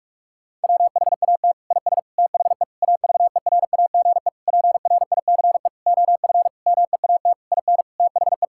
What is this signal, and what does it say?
Morse code, heard as a keyed tone, about 46 wpm, 700 Hz. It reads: WHAT IS THE AVERAGE PRICE OF MEAT IN THE
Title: Sentences from Top 1000 Words - Review Part 6 - 45wpm